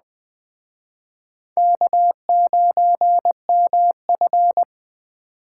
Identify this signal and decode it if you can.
Morse code, heard as a keyed tone, about 20 wpm, 700 Hz. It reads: K9MF
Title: Speed Racing - USA 1x2 Call Signs 17wpm